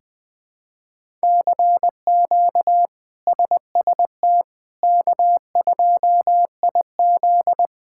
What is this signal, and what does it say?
Morse code, heard as a keyed tone, about 20 wpm, 700 Hz. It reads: CQ SST K2IZ
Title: K1USN SST QSO Practice - 20wpm